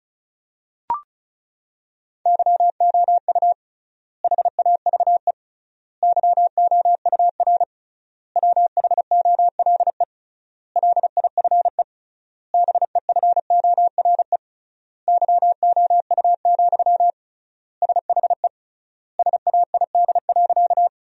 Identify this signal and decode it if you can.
Morse code, heard as a keyed tone, about 36 wpm, 700 Hz. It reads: YOU HAVE YOUR WHOLE LIFE BEFORE YOU, SHE SAID.
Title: 3x Word-Spacing - Sentences from Top 400 Words - 35wpm